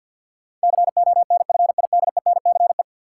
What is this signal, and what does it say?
Morse code, heard as a keyed tone, about 53 wpm, 700 Hz. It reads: CONFIDENCE
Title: Sentences from Top 1500 Words - New Word Drill - 50wpm